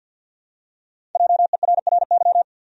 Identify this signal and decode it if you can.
Morse code, heard as a keyed tone, about 52 wpm, 700 Hz. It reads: JERRY
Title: USA Common Names - Max 5 Letters - 50wpm